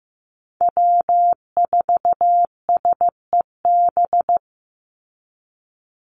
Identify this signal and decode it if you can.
Morse code, heard as a keyed tone, about 15 wpm, 700 Hz. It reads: W4SEB
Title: USA Call Signs 15wpm